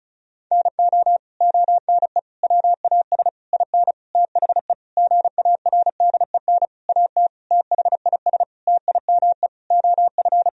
Morse code, heard as a keyed tone, about 36 wpm, 700 Hz. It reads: NO ONE WAS IN THE GARDEN AT THIS TIME OF
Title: Sentences from Top 1000 Words - Review Part 4 - 35wpm